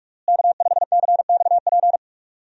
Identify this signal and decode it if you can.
Morse code, heard as a keyed tone, about 46 wpm, 700 Hz. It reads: K5CXP
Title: USA Call Signs - 5 Characters - Repeated 3x - 45wpm